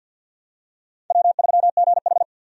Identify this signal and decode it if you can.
Morse code, heard as a keyed tone, about 50 wpm, 700 Hz. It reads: W3GH
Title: USA Call Signs - 4 Characters - 50wpm